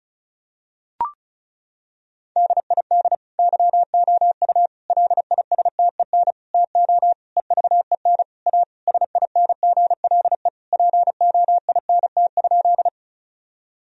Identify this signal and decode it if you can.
Morse code, heard as a keyed tone, about 36 wpm, 700 Hz. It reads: DID YOU LISTEN TO EVEN A SINGLE POINT?
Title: Sentences from Top 1000 Words - Review Part 4 - 35wpm